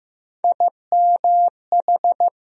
Morse code, heard as a keyed tone, about 15 wpm, 700 Hz. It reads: IMH
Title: USA Call Sign Postfixes - 3 Letters 15wpm